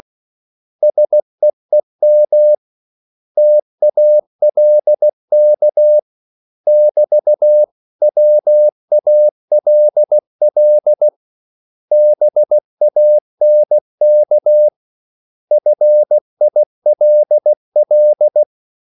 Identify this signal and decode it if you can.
Morse code, heard as a keyed tone, about 16 wpm, 600 Hz. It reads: SEEM TALK <BT> WALL BANK FILL
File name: CW / Practice / 500MostCommonEnglishWords16